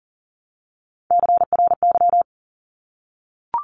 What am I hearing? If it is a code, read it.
Morse code, heard as a keyed tone, about 40 wpm, 700 Hz. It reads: CRY
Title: Top 500 Words Repeated 40wpm